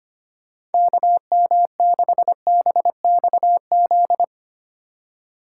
Morse code, heard as a keyed tone, about 25 wpm, 700 Hz. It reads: KM6BXZ
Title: USA Call Signs - 4x Character Spacing - 25wpm